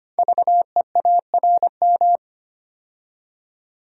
Morse code, heard as a keyed tone, about 25 wpm, 700 Hz. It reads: VEARM
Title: Code Groups - 5 Letters - 25wpm